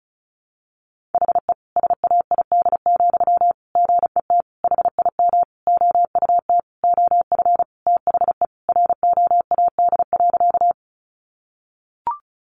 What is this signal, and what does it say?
Morse code, heard as a keyed tone, about 34 wpm, 700 Hz. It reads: HE SAID, GET HIM OUT OF THE ROAD.
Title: Sentences from Top 400 Words 35wpm